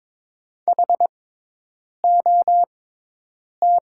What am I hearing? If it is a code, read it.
Morse code, heard as a keyed tone, about 22 wpm, spaced as for 8 wpm, 700 Hz. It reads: HOT
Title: 6x Character Spacing - Top 200 Words - 22wpm